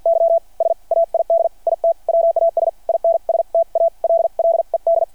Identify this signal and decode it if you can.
Morse code, heard as a keyed tone, about 53 wpm, spaced as for 43 wpm, 690 Hz. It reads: Y SAID IT WAS INSTALLED
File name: archived / CRSnet